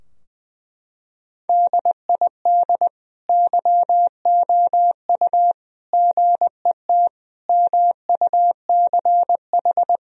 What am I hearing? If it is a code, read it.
Morse code, heard as a keyed tone, about 21 wpm, 700 Hz. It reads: DID YOU GET MUCH
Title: Sentences from Top 1000 Words - Review Part 3 - 20wpm